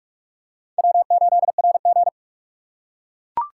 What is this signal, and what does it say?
Morse code, heard as a keyed tone, about 42 wpm, 700 Hz. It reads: W8RG
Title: USA Call Signs - 3x Character Spacing - 45wpm